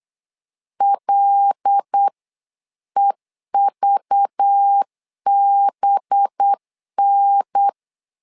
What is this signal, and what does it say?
Morse code, heard as a keyed tone, about 8 wpm, 800 Hz. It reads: LEVBN
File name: LEZIONI / 5 LBVC / 5 LBV 8wpm 2837